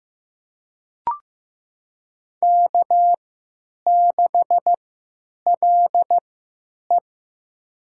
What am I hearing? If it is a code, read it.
Morse code, heard as a keyed tone, about 15 wpm, spaced as for 9 wpm, 700 Hz. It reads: K6LE
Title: USA Call Signs - 4 Characters - 3x Character Spacing - 15wpm